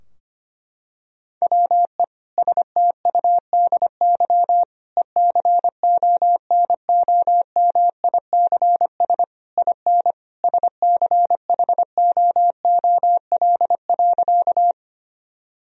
Morse code, heard as a keyed tone, about 25 wpm, 700 Hz. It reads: WE STUDY ECONOMICS IN SCHOOL.
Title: Sentences from Top 1000 Words - Review Part 3 - 25wpm